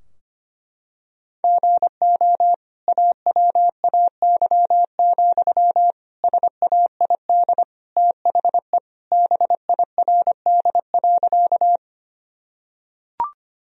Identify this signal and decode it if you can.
Morse code, heard as a keyed tone, about 25 wpm, 700 Hz. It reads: GO AWAY, SAID THE BIRD.
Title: Sentences from Top 1000 Words - Review Part 2 - 25wpm